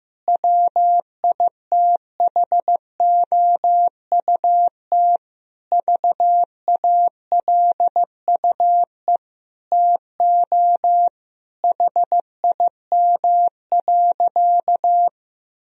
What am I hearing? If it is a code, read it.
Morse code, heard as a keyed tone, about 16 wpm, 700 Hz. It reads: WITHOUT VALUE TO HIM.
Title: Sentences from Top 1000 Words - Review Part 3 - 15wpm